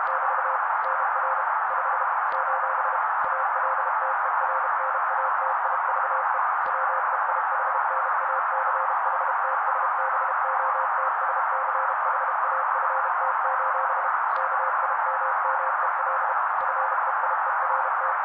Morse code, heard as a keyed tone, about 32 wpm, 530 Hz. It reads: <BT> NR 5 : ALTERNATIVE WEIHNACHTSBOTSCHAFT: KIMMEL FEIERT
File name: daily / 31wpm noise